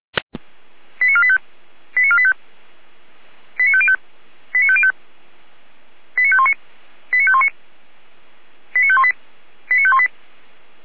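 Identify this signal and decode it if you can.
Five-tone selective call (ZVEI), heard as quick runs of five tones, tones of about 70 ms. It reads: 89485 89485 89505 89505 89519 89519 89518 89518
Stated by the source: ZVEI five-tone call received by SDR, fire brigade Orlamünde.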